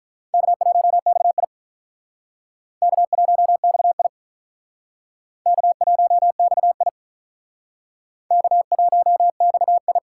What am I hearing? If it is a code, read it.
Morse code, heard as a keyed tone, about 46 wpm, 700 Hz. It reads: K1XI K1XI K1XI K1XI